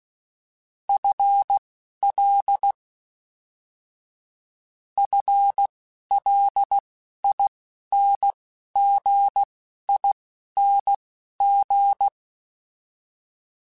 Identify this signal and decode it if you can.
Morse code, heard as a keyed tone, about 16 wpm, spaced as for 11 wpm, 800 Hz. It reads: FL FLINGING
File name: FL / FL mp3s / FL Prefixes 1x1 @10